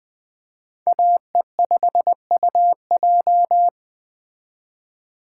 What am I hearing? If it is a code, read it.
Morse code, heard as a keyed tone, about 20 wpm, 700 Hz. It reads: AE5UJ
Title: Speed Racing - USA 2x2 Call Signs 17wpm